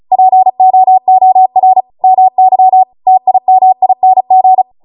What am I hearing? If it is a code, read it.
Morse code, heard as a keyed tone, about 36 wpm, 750 Hz. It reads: POOR MY TIMING